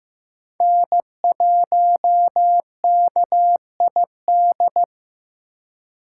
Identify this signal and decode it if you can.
Morse code, heard as a keyed tone, about 15 wpm, 700 Hz. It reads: N1KID
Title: USA Call Signs - 5 Characters - 2x Character Spacing - 15wpm